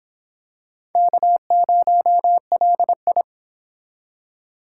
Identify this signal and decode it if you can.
Morse code, heard as a keyed tone, about 26 wpm, 700 Hz. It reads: K0LI